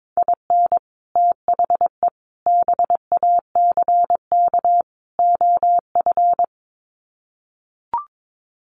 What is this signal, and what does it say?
Morse code, heard as a keyed tone, about 22 wpm, 700 Hz. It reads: IN THE BACK OF
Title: Sets of 4 Words 22wpm